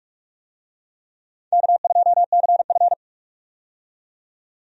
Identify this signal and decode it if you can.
Morse code, heard as a keyed tone, about 42 wpm, 700 Hz. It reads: K2CF
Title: USA Call Signs - 4 Characters - 45wpm